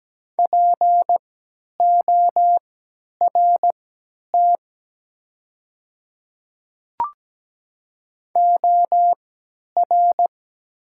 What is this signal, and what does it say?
Morse code, heard as a keyed tone, about 17 wpm, spaced as for 10 wpm, 700 Hz. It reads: PORT OR
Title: 3x Character Spacing - Top 200 Words - 17wpm